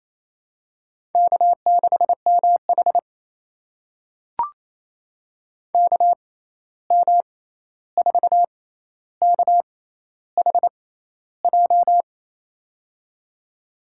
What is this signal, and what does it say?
Morse code, heard as a keyed tone, about 28 wpm, spaced as for 13 wpm, 700 Hz. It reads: K6MH KM4KHJ
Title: USA Call Signs - 6x Character Spacing - 28wpm